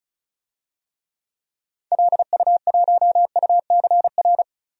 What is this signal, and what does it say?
Morse code, heard as a keyed tone, about 36 wpm, 700 Hz. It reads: LU1UCR
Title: DXCC-100 Call Signs - 6 Characters - 35wpm